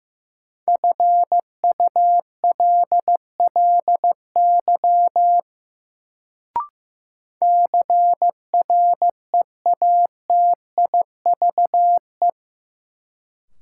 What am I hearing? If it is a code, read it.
Morse code, heard as a keyed tone, about 16 wpm, 700 Hz. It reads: FULLY CREATIVE
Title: Sentences from Top 1300 Words - New Word Drill Alt - 15wpm